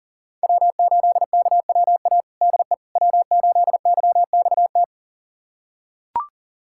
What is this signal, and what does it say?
Morse code, heard as a keyed tone, about 40 wpm, 700 Hz. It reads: W8KWA DE W8YXT
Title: USA Callsign DE Callsign - 5 Characters - 40wpm